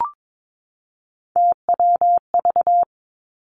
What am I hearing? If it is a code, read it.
Morse code, heard as a keyed tone, about 22 wpm, 700 Hz. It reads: TWV
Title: Three Challenging Letter-Number Combinations 22wpm